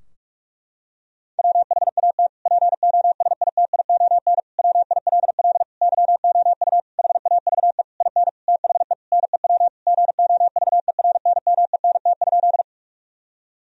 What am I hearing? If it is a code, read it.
Morse code, heard as a keyed tone, about 46 wpm, 700 Hz. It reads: WHAT POSITION WILL YOU HAVE IN THE NEW GOVERNMENT?
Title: Sentences from Top 1000 Words - Review Part 3 - 45wpm